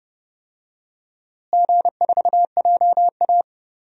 Morse code, heard as a keyed tone, about 30 wpm, 700 Hz. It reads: G4JA
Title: DXCC-100 Call Signs - 4 Characters - Repeated 2x - 30wpm